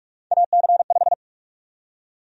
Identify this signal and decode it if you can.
Morse code, heard as a keyed tone, about 46 wpm, 700 Hz. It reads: AC5
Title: USA Call Sign Prefixes - 45wpm